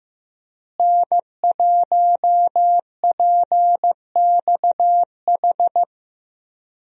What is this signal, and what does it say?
Morse code, heard as a keyed tone, about 15 wpm, 700 Hz. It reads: N1PXH
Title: USA Call Signs - 5 Characters - Repeated 3x - 15wpm